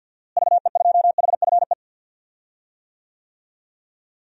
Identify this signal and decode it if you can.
Morse code, heard as a keyed tone, about 52 wpm, 700 Hz. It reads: VE2HFE
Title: DXCC-100 Call Signs - 6 Characters - Repeated 2x - 50wpm